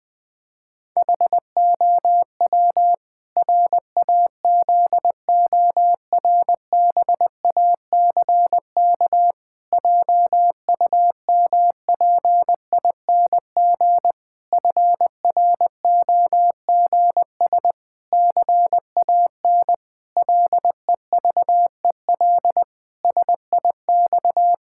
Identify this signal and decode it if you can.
Morse code, heard as a keyed tone, about 20 wpm, 700 Hz. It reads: HOW RAZORBACK JUMPING FROGS CAN LEVEL SIX
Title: Pangram Challenge v2 - 20wpm